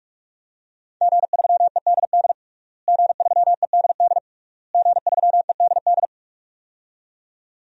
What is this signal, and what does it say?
Morse code, heard as a keyed tone, about 47 wpm, 700 Hz. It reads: G3EDD G3EDD G3EDD
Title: DXCC-100 Call Signs - 5 Characters - Repeated 3x - 45wpm